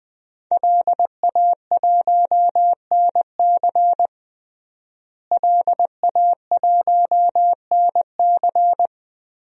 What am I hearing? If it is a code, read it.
Morse code, heard as a keyed tone, about 20 wpm, 700 Hz. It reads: LA1NC LA1NC
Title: DXCC-100 Call Signs - 5 Characters - Repeated 2x - 20wpm